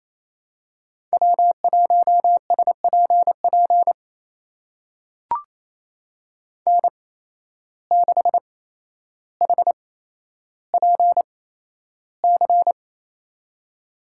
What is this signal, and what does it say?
Morse code, heard as a keyed tone, about 28 wpm, spaced as for 12 wpm, 700 Hz. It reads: W1SPP N6HPC
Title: USA Call Signs - 5 Characters - 8x Character Spacing - 28wpm